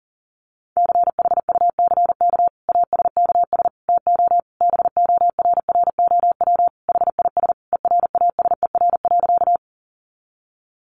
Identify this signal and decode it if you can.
Morse code, heard as a keyed tone, about 40 wpm, 700 Hz. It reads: CHUCK ASKS TO BORROW HIS ERASER.
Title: Additional Easy Phrases 40wpm